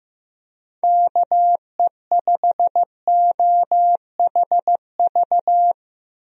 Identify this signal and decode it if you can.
Morse code, heard as a keyed tone, about 16 wpm, 700 Hz. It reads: KE5OHV
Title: USA Call Signs - 6 Characters - Repeated 3x - 15wpm